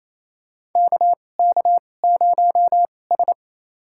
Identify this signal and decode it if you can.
Morse code, heard as a keyed tone, about 28 wpm, spaced as for 20 wpm, 700 Hz. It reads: KK0S
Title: USA Call Signs - 4 Characters - 2x Character Spacing - 28wpm